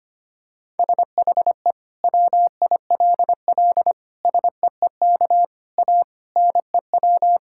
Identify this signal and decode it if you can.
Morse code, heard as a keyed tone, about 25 wpm, 700 Hz. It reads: SHE WILL SEEK A NEW
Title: Sentences from Top 1600 Words 25wpm